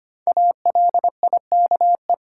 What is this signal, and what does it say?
Morse code, heard as a keyed tone, about 25 wpm, 700 Hz. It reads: ALIKE
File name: 5 Letter Words - First 3 Letters Given - 25wpm